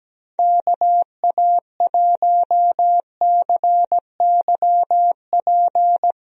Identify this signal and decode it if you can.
Morse code, heard as a keyed tone, about 17 wpm, 700 Hz. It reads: KA1CYP
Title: USA Call Signs - 6 Characters - 17wpm